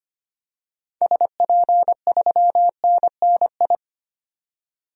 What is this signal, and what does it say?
Morse code, heard as a keyed tone, about 25 wpm, 700 Hz. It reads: SP3NNI
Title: DXCC-100 Call Signs - 6 Characters - 25wpm